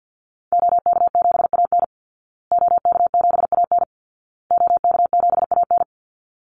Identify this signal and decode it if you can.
Morse code, heard as a keyed tone, about 52 wpm, 700 Hz. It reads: OK7AN OK7AN OK7AN
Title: Call Signs Repeated 2x - 50wpm